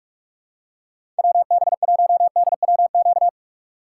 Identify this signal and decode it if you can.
Morse code, heard as a keyed tone, about 42 wpm, 700 Hz. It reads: WB1DWQ
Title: USA Call Signs - 6 Characters - 4x Character Spacing - 45wpm